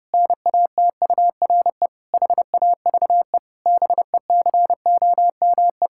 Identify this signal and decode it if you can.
Morse code, heard as a keyed tone, about 30 wpm, 700 Hz. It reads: NATURE HAVE BECOME
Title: Pangram Challenge v2 - 30wpm